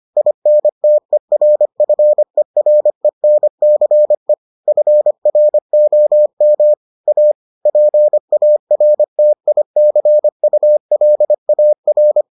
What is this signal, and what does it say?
Morse code, heard as a keyed tone, about 25 wpm, 600 Hz. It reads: INTERFERENCE FROM A PARTICULAR